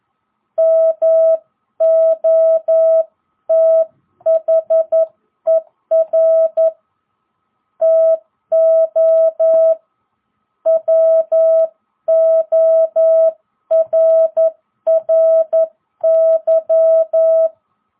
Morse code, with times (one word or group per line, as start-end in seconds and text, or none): 0.6-6.7 MOTHER
7.8-9.7 TO
10.7-17.5 WORRY